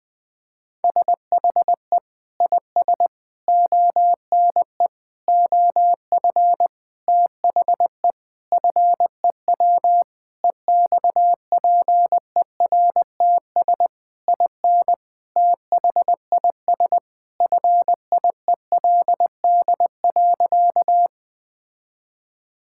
Morse code, heard as a keyed tone, about 21 wpm, 700 Hz. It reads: SHE IS ONE OF THE FEW EXPERTS IN THIS FIELD.
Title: Sentences with 5-Word Phrases - 20wpm